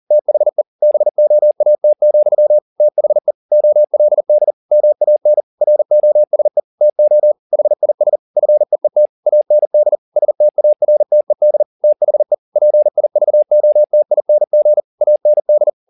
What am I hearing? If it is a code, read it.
Morse code, heard as a keyed tone, about 40 wpm, 600 Hz. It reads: THE BOAT, THE OLD MAN ROSE TO HIS FEET AND STARTED THE PIVOTING AND
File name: the old man and the sea 2 40wpm